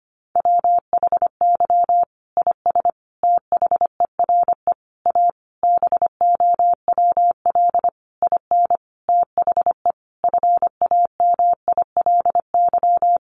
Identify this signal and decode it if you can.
Morse code, heard as a keyed tone, about 25 wpm, 700 Hz. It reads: WHY IS THERE A BOWL IN THE FAMILY
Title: Sentences from Top 1200 Words 25wpm